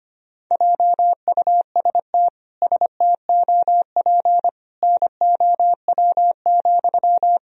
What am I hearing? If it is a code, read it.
Morse code, heard as a keyed tone, about 25 wpm, 700 Hz. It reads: JUST STOP NOW,
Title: Sentences from Top 1000 Words - Review Part 1 - 25wpm